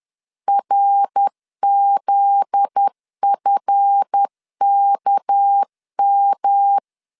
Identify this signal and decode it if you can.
Morse code, heard as a keyed tone, about 10 wpm, 800 Hz. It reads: RZFKM
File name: LEZIONI / 9 2 10wpm 2736